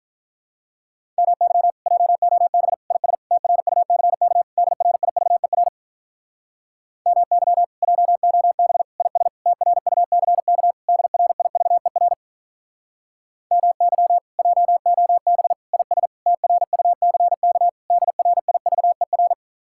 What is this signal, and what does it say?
Morse code, heard as a keyed tone, about 46 wpm, 700 Hz. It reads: MY JOB IS TRUCK DRIVER MY JOB IS TRUCK DRIVER MY JOB IS TRUCK DRIVER